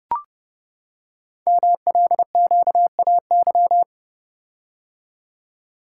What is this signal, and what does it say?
Morse code, heard as a keyed tone, about 30 wpm, 700 Hz. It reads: MLQAY
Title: Code Groups - 5 Letters - 30wpm